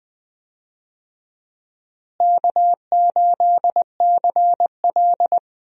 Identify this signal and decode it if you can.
Morse code, heard as a keyed tone, about 20 wpm, 700 Hz. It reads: K8CL
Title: USA Call Signs - 4 Characters - Repeated 2x - 20wpm